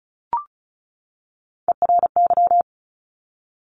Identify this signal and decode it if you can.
Morse code, heard as a keyed tone, about 36 wpm, 700 Hz. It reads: ERY
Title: Top Three Letter Combinations in Words 35wpm